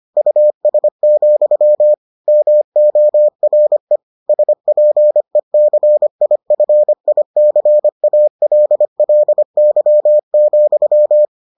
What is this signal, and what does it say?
Morse code, heard as a keyed tone, about 25 wpm, 600 Hz. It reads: US, MORE SPECIFICALLY,